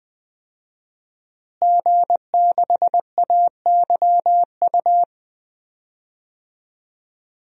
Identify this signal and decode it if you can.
Morse code, heard as a keyed tone, about 20 wpm, 700 Hz. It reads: G6AYU
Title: DXCC-100 Call Signs - 5 Characters - Repeated 3x - 20wpm